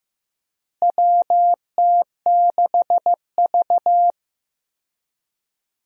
Morse code, heard as a keyed tone, about 15 wpm, 700 Hz. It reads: WT6V
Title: USA Call Signs - 4 Characters - Repeated 3x - 15wpm